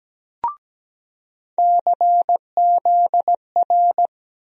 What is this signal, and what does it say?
Morse code, heard as a keyed tone, about 17 wpm, 700 Hz. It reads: CZR
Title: USA Call Sign Postfixes - 3 Letters 17wpm